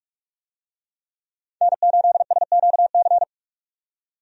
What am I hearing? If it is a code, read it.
Morse code, heard as a keyed tone, about 42 wpm, 700 Hz. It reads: N8SQC
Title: USA Call Signs - 5 Characters - 45wpm